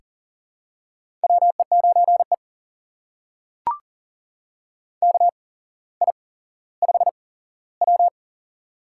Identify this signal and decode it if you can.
Morse code, heard as a keyed tone, about 40 wpm, spaced as for 17 wpm, 700 Hz. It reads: WE9E KI5W
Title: USA Call Signs - 4 Characters - 8x Character Spacing - 40wpm